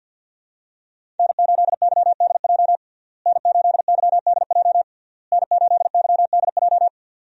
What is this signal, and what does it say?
Morse code, heard as a keyed tone, about 50 wpm, 700 Hz. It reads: N8YDJ N8YDJ N8YDJ
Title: USA Call Signs - 5 Characters - Repeated 3x - 50wpm